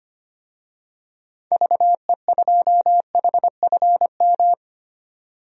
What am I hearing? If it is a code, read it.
Morse code, heard as a keyed tone, about 25 wpm, 700 Hz. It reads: VE2HFM
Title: DXCC-100 Call Signs - 6 Characters - 25wpm